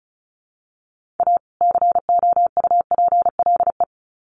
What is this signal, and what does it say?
Morse code, heard as a keyed tone, about 34 wpm, 700 Hz. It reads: A COUPLE
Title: Sets of 2 Words 35wpm